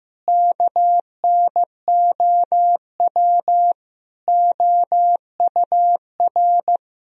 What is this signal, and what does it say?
Morse code, heard as a keyed tone, about 14 wpm, 700 Hz. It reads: KNOW OUR
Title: Sentences from Top 1000 Words - Review Part 4 - 15wpm